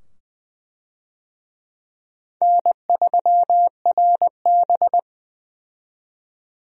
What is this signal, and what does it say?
Morse code, heard as a keyed tone, about 20 wpm, 700 Hz. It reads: N3RB
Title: USA Call Signs - 4 Characters - 20wpm